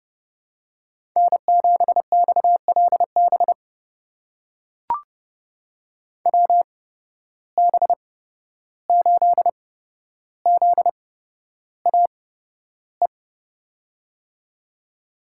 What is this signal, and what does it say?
Morse code, heard as a keyed tone, about 31 wpm, spaced as for 12 wpm, 700 Hz. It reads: N7XLB WB8ZAE
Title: USA Call Signs - 8x Character Spacing - 30wpm